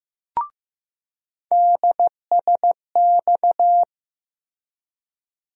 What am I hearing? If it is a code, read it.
Morse code, heard as a keyed tone, about 15 wpm, 700 Hz. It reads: DSX
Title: USA Call Sign Postfixes - 3 Letters 15wpm